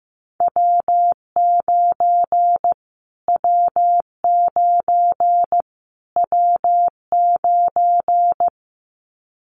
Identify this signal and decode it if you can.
Morse code, heard as a keyed tone, about 15 wpm, 700 Hz. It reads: W9 W9 W9
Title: Callsign Prefixes for Top 25 DXCC - Repeated 3x - 15wpm